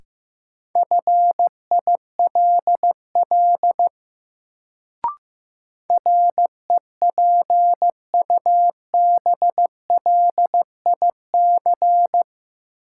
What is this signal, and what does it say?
Morse code, heard as a keyed tone, about 15 wpm, 700 Hz. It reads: FILL REPUBLIC
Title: Sentences from Top 1500 Words - New Word Drill Alt - 15wpm